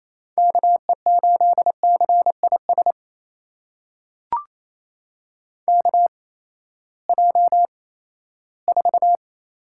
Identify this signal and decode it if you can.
Morse code, heard as a keyed tone, about 28 wpm, spaced as for 16 wpm, 700 Hz. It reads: KE8CIS KJ4
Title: USA Call Signs - 8x Character Spacing - 28wpm